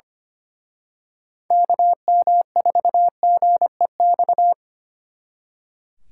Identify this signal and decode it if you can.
Morse code, heard as a keyed tone, about 31 wpm, 700 Hz. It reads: KM4GEX
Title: Speed Racing - USA 2x3 Call Signs 25wpm